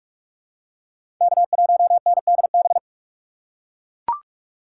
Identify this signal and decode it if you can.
Morse code, heard as a keyed tone, about 41 wpm, 700 Hz. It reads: K1NDB
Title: USA Call Signs - 5 Characters - 2x Character Spacing - 45wpm